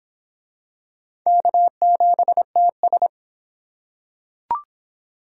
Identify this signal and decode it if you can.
Morse code, heard as a keyed tone, about 26 wpm, 700 Hz. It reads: K7TS